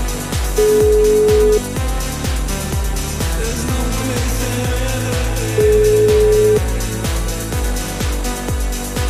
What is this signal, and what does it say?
Ringback tone (Europe, 425 Hz), with noise.